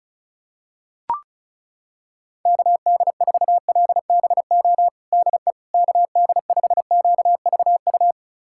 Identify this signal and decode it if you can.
Morse code, heard as a keyed tone, about 36 wpm, 700 Hz. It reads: KD4LBO DE KD5QVU
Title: USA Callsign DE Callsign - 6 Characters - 35wpm